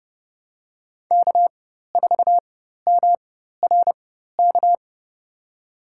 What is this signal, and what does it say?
Morse code, heard as a keyed tone, about 30 wpm, spaced as for 14 wpm, 700 Hz. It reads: K4MRK
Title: USA Call Signs - 5 Characters - 4x Character Spacing - 30wpm